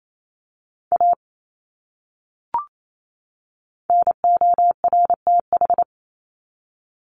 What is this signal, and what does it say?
Morse code, heard as a keyed tone, about 28 wpm, 700 Hz. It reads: A NORTH
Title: Top 300 Words 28wpm